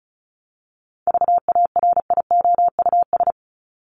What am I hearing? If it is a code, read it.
Morse code, heard as a keyed tone, about 34 wpm, 700 Hz. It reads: VARIOUS
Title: Top 501-750 Words 35wpm